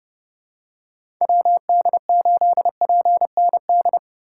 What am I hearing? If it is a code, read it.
Morse code, heard as a keyed tone, about 30 wpm, 700 Hz. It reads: WD8PND
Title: USA Call Signs - 6 Characters - 3x Character Spacing - 30wpm